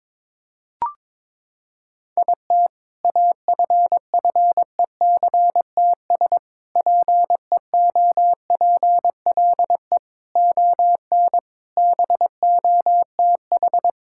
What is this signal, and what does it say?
Morse code, heard as a keyed tone, about 22 wpm, 700 Hz. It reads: IT AFFECTS PEOPLE ON BOTH
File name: Sentences with 7-Word Phrases - 22wpm